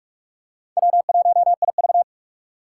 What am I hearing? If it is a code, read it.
Morse code, heard as a keyed tone, about 43 wpm, 700 Hz. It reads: W1IV
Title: Four Challenging Letter-Number Combinations 45wpm